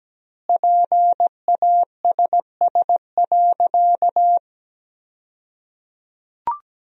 Morse code, heard as a keyed tone, about 17 wpm, 700 Hz. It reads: PASS.
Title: 2x Word-Spacing - Sentences from Top 400 Words - 17wpm